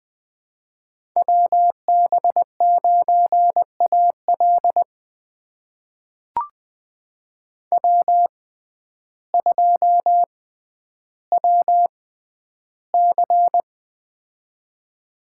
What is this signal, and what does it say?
Morse code, heard as a keyed tone, about 20 wpm, spaced as for 11 wpm, 700 Hz. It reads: WB9AL W2WC
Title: USA Call Signs - 6x Character Spacing - 20wpm